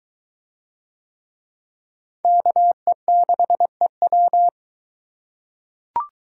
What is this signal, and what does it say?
Morse code, heard as a keyed tone, about 21 wpm, 700 Hz. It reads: KE6EW